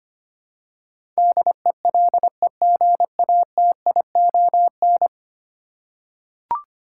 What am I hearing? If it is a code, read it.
Morse code, heard as a keyed tone, about 25 wpm, 700 Hz. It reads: DELEGATION
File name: Ten Letter Words - 25wpm